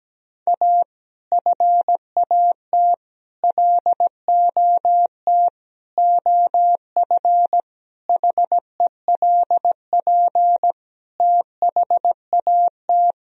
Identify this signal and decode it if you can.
Morse code, heard as a keyed tone, about 17 wpm, 700 Hz. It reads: A FAT LOT OF HELP THAT
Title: Sentences from Top 1000 Words - Review Part 4 - 17wpm